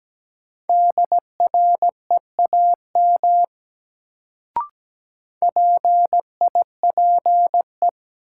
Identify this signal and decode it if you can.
Morse code, heard as a keyed tone, about 17 wpm, 700 Hz. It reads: DREAM PIPE
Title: Sentences from Top 1500 Words - New Word Drill Alt - 17wpm